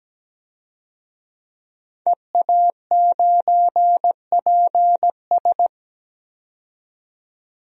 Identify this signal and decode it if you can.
Morse code, heard as a keyed tone, about 17 wpm, 700 Hz. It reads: EA9PS
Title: DXCC-100 Call Signs - 5 Characters - 17wpm